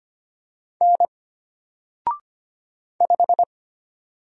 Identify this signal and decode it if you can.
Morse code, heard as a keyed tone, about 25 wpm, 700 Hz. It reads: N 5
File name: Single-Letter-Number-25wpm